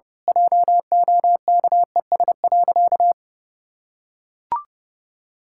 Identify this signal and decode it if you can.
Morse code, heard as a keyed tone, about 30 wpm, 700 Hz. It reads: JOKES.
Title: Pangram Challenge v2 - 30wpm